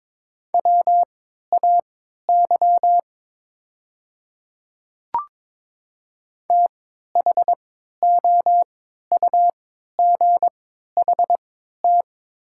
Morse code, heard as a keyed tone, about 22 wpm, spaced as for 12 wpm, 700 Hz. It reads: WAY THOUGHT
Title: 3x Character Spacing - Top 300 Words - 22wpm